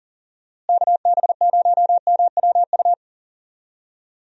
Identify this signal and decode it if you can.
Morse code, heard as a keyed tone, about 37 wpm, 700 Hz. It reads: KB0MWU
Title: USA Call Signs - 6 Characters - Repeated 3x - 40wpm